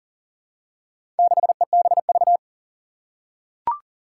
Morse code, heard as a keyed tone, about 38 wpm, 700 Hz. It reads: 6EBV
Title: Four Challenging Letter-Number Combinations 40wpm